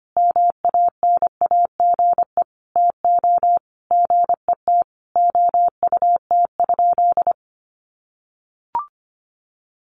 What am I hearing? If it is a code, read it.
Morse code, heard as a keyed tone, about 25 wpm, 700 Hz. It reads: MANAGE TO GET OUT?
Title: Sentences from Top 1400 Words 25wpm